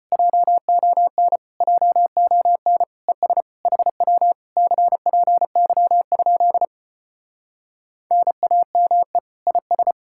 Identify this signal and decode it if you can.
Morse code, heard as a keyed tone, about 33 wpm, 700 Hz. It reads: JON JON ES HW CPY? NAME IS